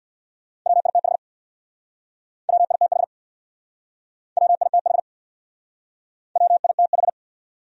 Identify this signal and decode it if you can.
Morse code, heard as a keyed tone, about 67 wpm, 700 Hz. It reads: WITH WITH WITH WITH